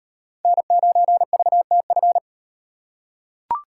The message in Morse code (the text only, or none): N9VTF